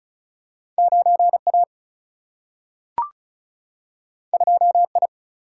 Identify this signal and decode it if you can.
Morse code, heard as a keyed tone, about 36 wpm, 700 Hz. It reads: 9 A 2 I
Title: Single-Letter Single-Number 35wpm